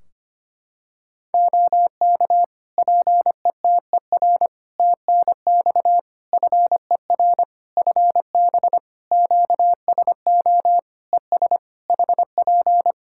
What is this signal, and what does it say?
Morse code, heard as a keyed tone, about 25 wpm, 700 Hz. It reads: OK PETER TNX FER FB QSO ES HP
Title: Practice QSO - Exchange 4 - Ending - 25wpm